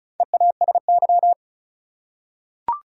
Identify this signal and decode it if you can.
Morse code, heard as a keyed tone, about 34 wpm, 700 Hz. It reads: EASY
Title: Four Letter Words - 35wpm